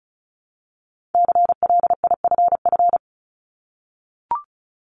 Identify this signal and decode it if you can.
Morse code, heard as a keyed tone, about 36 wpm, 700 Hz. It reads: CLIFF
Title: Multi-letter Phonemes 35wpm